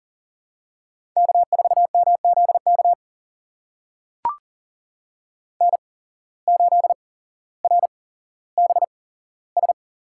Morse code, heard as a keyed tone, about 42 wpm, spaced as for 17 wpm, 700 Hz. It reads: K4MZK N8RBS
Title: USA Call Signs - 5 Characters - 8x Character Spacing - 40wpm